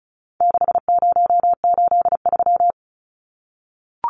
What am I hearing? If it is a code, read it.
Morse code, heard as a keyed tone, about 34 wpm, 700 Hz. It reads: 6083
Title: Numbers - 4-Digit 35wpm